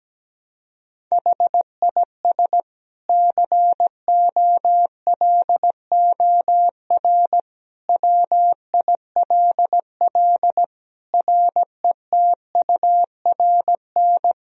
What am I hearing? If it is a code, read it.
Morse code, heard as a keyed tone, about 17 wpm, 700 Hz. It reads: HIS COLOR WILL RETURN
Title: Sentences from Top 1000 Words - Review Part 2 - 17wpm